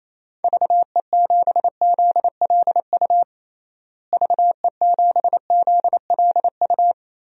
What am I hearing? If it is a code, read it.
Morse code, heard as a keyed tone, about 28 wpm, 700 Hz. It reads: VE7ZLU VE7ZLU
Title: DXCC-100 Call Signs - 6 Characters - Repeated 3x - 28wpm